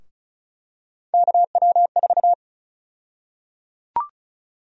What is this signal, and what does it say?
Morse code, heard as a keyed tone, about 36 wpm, 700 Hz. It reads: KW4
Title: USA Call Sign Prefixes - 35wpm